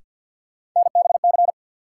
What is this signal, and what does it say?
Morse code, heard as a keyed tone, about 53 wpm, 700 Hz. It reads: NBC
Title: Sets of 5 Words Encore - New Word Drill - 50wpm